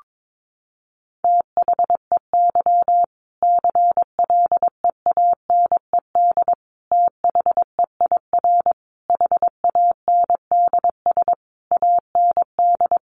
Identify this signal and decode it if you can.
Morse code, heard as a keyed tone, about 22 wpm, 700 Hz. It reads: THEY CLEANED THEIR HANDS AND
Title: A Sound Of Thunder 22wpm